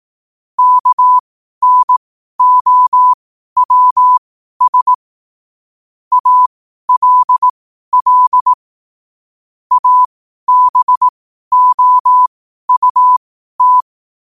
Morse code, heard as a keyed tone, about 18 wpm, spaced as for 12 wpm, 1000 Hz. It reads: KNOWS ALL ABOUT